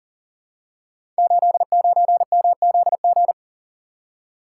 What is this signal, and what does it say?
Morse code, heard as a keyed tone, about 38 wpm, 700 Hz. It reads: 89MZG
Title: Code Groups - 5 Letter:Numbers - 40wpm